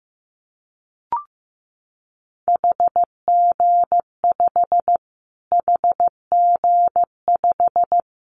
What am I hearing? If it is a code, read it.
Morse code, heard as a keyed tone, about 15 wpm, 700 Hz. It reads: HG5 HG5
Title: Callsign Prefixes for Top 25 DXCC - Repeated 3x - 15wpm